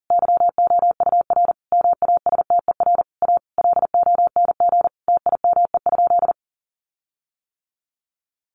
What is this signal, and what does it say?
Morse code, heard as a keyed tone, about 40 wpm, 700 Hz. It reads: YOUR MASTER A LONG TIME?
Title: Sentences from Top 1300 Words 40wpm